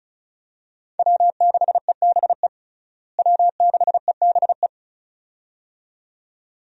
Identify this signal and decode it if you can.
Morse code, heard as a keyed tone, about 34 wpm, 700 Hz. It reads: W6EBE W6EBE
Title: USA Call Signs - 5 Characters - Repeated 2x - 35wpm